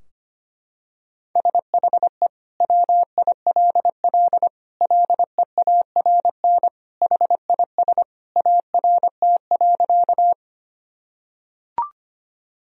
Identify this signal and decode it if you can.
Morse code, heard as a keyed tone, about 25 wpm, 700 Hz. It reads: SHE WILL LEARN HIS ART.
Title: Sentences from Top 1000 Words - Review Part 3 - 25wpm